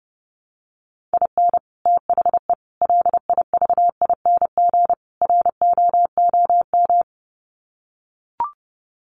Morse code, heard as a keyed tone, about 30 wpm, 700 Hz. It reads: IN THE LIVING ROOM
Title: Sets of 4 Words 30wpm